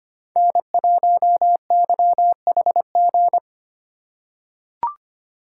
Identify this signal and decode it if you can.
Morse code, heard as a keyed tone, about 25 wpm, 700 Hz. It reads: N1YHG
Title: USA Call Signs - 5 Characters - 3x Character Spacing - 25wpm